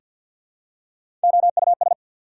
Morse code, heard as a keyed tone, about 48 wpm, 690 Hz. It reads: OUS
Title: Prefixes and Suffixes 50wpm